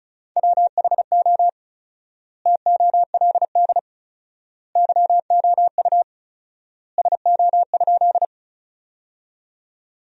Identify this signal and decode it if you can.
Morse code, heard as a keyed tone, about 34 wpm, 700 Hz. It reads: WHO TOLD YOU SO?
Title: 4x Word-Spacing - Sentences from Top 400 Words - 35wpm